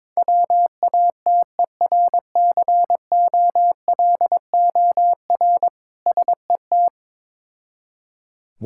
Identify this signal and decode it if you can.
Morse code, heard as a keyed tone, about 22 wpm, 690 Hz. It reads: WATERCOLOR SET
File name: Difficult Words with Hints - 22wpm